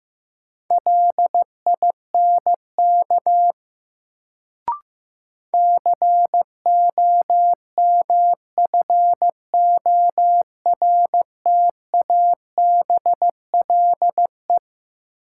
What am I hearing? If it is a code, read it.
Morse code, heard as a keyed tone, about 15 wpm, 700 Hz. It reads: LINK COMFORTABLE
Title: Sentences from Top 1000 Words - Review Part 7 - New Word Drill - 15wpm